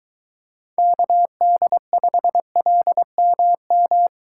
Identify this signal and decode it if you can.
Morse code, heard as a keyed tone, about 23 wpm, 700 Hz. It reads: KD5LMM